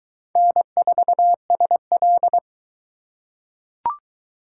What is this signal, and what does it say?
Morse code, heard as a keyed tone, about 22 wpm, 700 Hz. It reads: N4SL